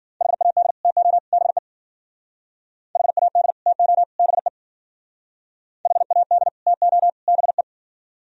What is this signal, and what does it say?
Morse code, heard as a keyed tone, about 48 wpm, 700 Hz. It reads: HAD TO BE HAD TO BE HAD TO BE
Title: Speed-Racing - Sets of 3 Words 40wpm